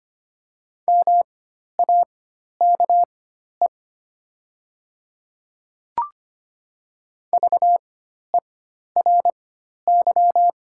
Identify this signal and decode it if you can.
Morse code, heard as a keyed tone, about 25 wpm, spaced as for 12 wpm, 700 Hz. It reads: MAKE VERY
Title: 4x Character Spacing - Top 300 Words - 25wpm